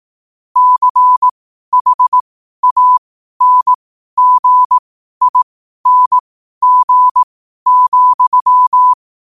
Morse code, heard as a keyed tone, about 18 wpm, spaced as for 12 wpm, 1000 Hz. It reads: CHANGING,
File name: 12wpm